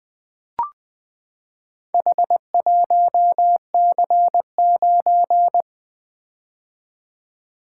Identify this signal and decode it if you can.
Morse code, heard as a keyed tone, about 20 wpm, 700 Hz. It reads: H1C9
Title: Four Letter-Number Combinations - 20wpm